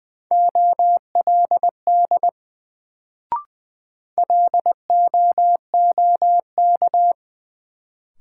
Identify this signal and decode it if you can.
Morse code, heard as a keyed tone, about 20 wpm, 700 Hz. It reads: OLD LOOK
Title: Sets of 2 Words - New Word Drill - 20wpm